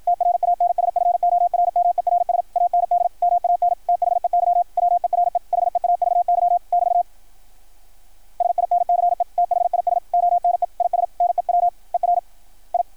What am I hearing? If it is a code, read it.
Morse code, heard as a keyed tone, about 53 wpm, 690 Hz. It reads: TRANSFORMERS AND MAN THEY WERE HEAVY <BT> SINCE THIS ONE IS NEWER I
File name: archived / CRSnet